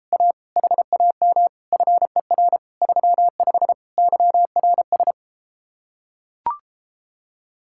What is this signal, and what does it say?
Morse code, heard as a keyed tone, about 33 wpm, 700 Hz. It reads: A HAM FER 35 YRS